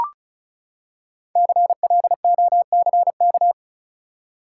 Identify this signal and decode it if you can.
Morse code, heard as a keyed tone, about 36 wpm, 700 Hz. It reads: CLOCK
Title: Five Letter Words - 35wpm